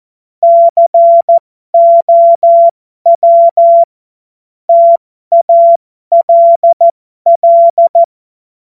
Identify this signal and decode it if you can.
Morse code, heard as a keyed tone, about 14 wpm, spaced as for 12 wpm, 680 Hz. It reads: COW TALL
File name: Morsekurs / Witze